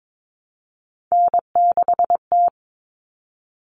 Morse code, heard as a keyed tone, about 22 wpm, 700 Hz. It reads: N6T
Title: All Cut Numbers - 3-Digit 22wpm